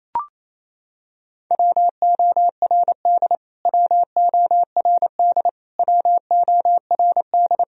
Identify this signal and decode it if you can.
Morse code, heard as a keyed tone, about 28 wpm, 700 Hz. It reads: WORD WORD WORD
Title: Top 100 Words Repeated 28wpm